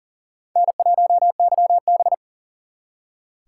Morse code, heard as a keyed tone, about 51 wpm, 700 Hz. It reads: N1YB